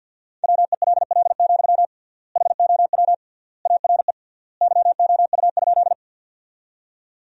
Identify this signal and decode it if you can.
Morse code, heard as a keyed tone, about 50 wpm, 700 Hz. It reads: WELL, HOW ARE YOU?
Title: 3x Word-Spacing - Sentences from Top 200 Words - 50wpm